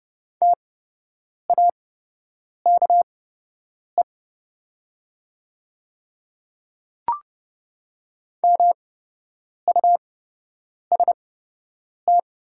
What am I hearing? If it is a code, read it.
Morse code, heard as a keyed tone, about 30 wpm, spaced as for 8 wpm, 700 Hz. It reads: TAKE MUST
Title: 8x Character Spacing - Top 200 Words - 30wpm